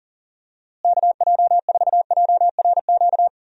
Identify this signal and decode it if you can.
Morse code, heard as a keyed tone, about 38 wpm, 700 Hz. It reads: KJ4JRQ
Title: USA Call Signs - 6 Characters - Repeated 2x - 40wpm